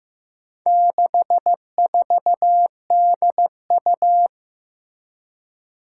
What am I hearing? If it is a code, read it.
Morse code, heard as a keyed tone, about 15 wpm, 700 Hz. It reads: 64DU
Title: Four Challenging Letter-Number Combinations 15wpm